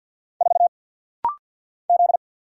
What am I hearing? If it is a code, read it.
Morse code, heard as a keyed tone, about 50 wpm, 690 Hz. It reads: 4 Z
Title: Single Letter-Number Alt - Rapid-Fire 50wpm